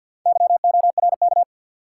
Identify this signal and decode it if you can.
Morse code, heard as a keyed tone, about 50 wpm, 700 Hz. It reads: YORK
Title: Sets of 2 Words - New Word Drill - 50wpm